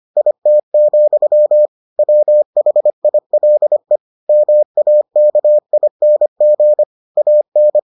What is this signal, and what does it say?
Morse code, heard as a keyed tone, about 25 wpm, 600 Hz. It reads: IT, WHILE MAKING AN